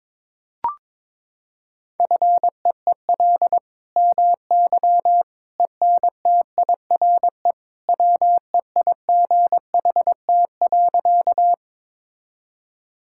Morse code, heard as a keyed tone, about 22 wpm, 700 Hz. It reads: FEEL MY ENTIRE WEIGHT.
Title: Sentences from Top 1000 Words - Review Part 5 - 22wpm